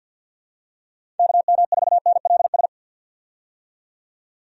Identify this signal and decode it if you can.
Morse code, heard as a keyed tone, about 51 wpm, 700 Hz. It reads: KM4NLS
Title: USA Call Signs - 6 Characters - 50wpm